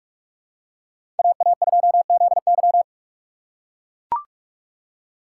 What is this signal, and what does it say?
Morse code, heard as a keyed tone, about 42 wpm, 700 Hz. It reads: AA2ZY
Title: USA Call Signs - 5 Characters - 4x Character Spacing - 45wpm